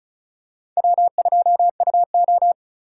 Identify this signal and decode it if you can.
Morse code, heard as a keyed tone, about 36 wpm, 700 Hz. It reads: W2UO
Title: USA Call Signs - 4 Characters - 3x Character Spacing - 35wpm